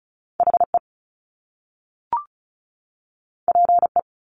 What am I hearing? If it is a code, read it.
Morse code, heard as a keyed tone, about 36 wpm, 700 Hz. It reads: HE PE
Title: Top Two Letter Combinations in Words 35wpm